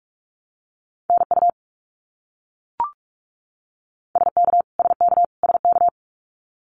Morse code, heard as a keyed tone, about 44 wpm, 700 Hz. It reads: NU SK SK SK
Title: Canadian Provinces and Territories - Repeated 3x - 45wpm